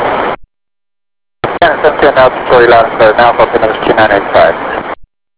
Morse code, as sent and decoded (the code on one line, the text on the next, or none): none